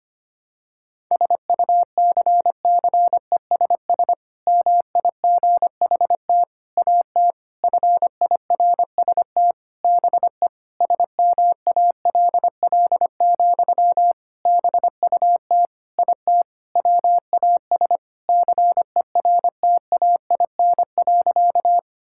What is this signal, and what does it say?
Morse code, heard as a keyed tone, about 25 wpm, 700 Hz. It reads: SUCCESS MIGHT AT FIRST BE SMALL, BUT IT WAS CERTAIN.
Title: Sentences from Top 1000 Words - Review Part 6 - 25wpm